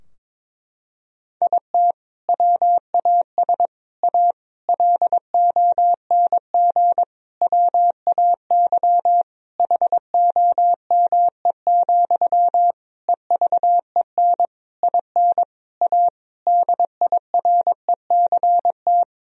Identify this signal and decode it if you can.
Morse code, heard as a keyed tone, about 22 wpm, 700 Hz. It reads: IT WAS A LONG WAY HOME, EVEN IN A DIRECT
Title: Sentences from Top 1000 Words - Review Part 2 - 22wpm